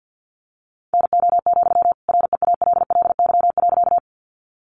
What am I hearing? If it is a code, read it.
Morse code, heard as a keyed tone, about 48 wpm, 700 Hz. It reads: NO, REALLY.
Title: Sentences from Top 600 Words 50wpm